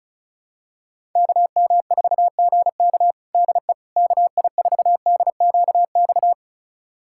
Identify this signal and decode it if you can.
Morse code, heard as a keyed tone, about 36 wpm, 700 Hz. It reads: KM4GK DE KI4DQX
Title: USA Callsign DE Callsign 35wpm